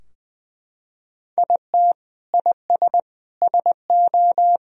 Morse code, heard as a keyed tone, about 20 wpm, 700 Hz. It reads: IT IS SO
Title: Sentences from Top 1000 Words - Review Part 3 - 20wpm